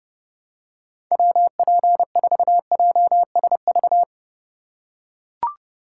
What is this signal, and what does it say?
Morse code, heard as a keyed tone, about 30 wpm, 700 Hz. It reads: WP4JSV
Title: USA Call Signs - 6 Characters - 2x Character Spacing - 30wpm